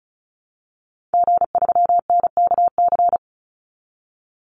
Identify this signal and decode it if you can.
Morse code, heard as a keyed tone, about 34 wpm, 700 Hz. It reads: G3NKC
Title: Call Signs Repeated 2x - 35wpm